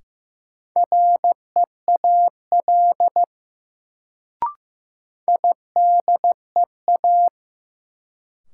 Morse code, heard as a keyed tone, about 15 wpm, 700 Hz. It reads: REAL IDEA
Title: Sentences from Top 300 Words - New Word Drill - 15wpm